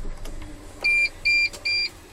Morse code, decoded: none